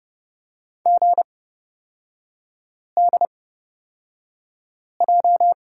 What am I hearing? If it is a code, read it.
Morse code, heard as a keyed tone, about 30 wpm, spaced as for 5 wpm, 700 Hz. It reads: GDJ
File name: Lesson 37e - Letter x - Three Characters 30wpm